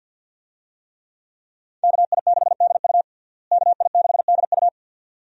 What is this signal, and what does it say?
Morse code, heard as a keyed tone, about 50 wpm, 700 Hz. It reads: KI6DU KI6DU
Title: USA Call Signs - 5 Characters - Repeated 2x - 50wpm